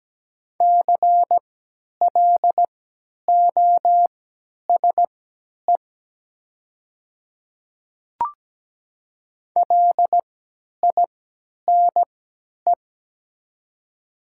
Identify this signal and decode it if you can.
Morse code, heard as a keyed tone, about 17 wpm, spaced as for 10 wpm, 700 Hz. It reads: CLOSE LINE
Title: 3x Character Spacing - Top 300 Words - 17wpm